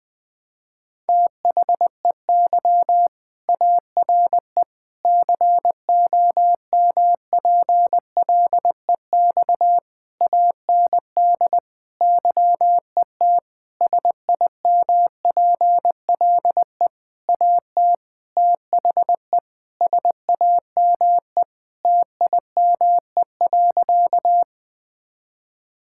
Morse code, heard as a keyed tone, about 20 wpm, 700 Hz. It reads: THEY ARE COMPLEX AND YET SIMPLE AT THE SAME TIME.
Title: Sentences from Top 1000 Words - Review Part 6 - 20wpm